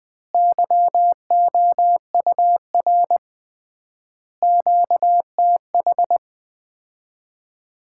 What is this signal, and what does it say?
Morse code, heard as a keyed tone, about 20 wpm, 700 Hz. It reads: YOUR QTH
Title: Common QSO Phrases - 3x Word-Spacing - 20wpm